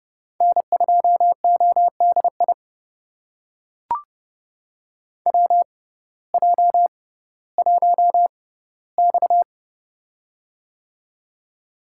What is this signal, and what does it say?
Morse code, heard as a keyed tone, about 30 wpm, spaced as for 17 wpm, 700 Hz. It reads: N2ODI WJ1X
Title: USA Call Signs - 6x Character Spacing - 30wpm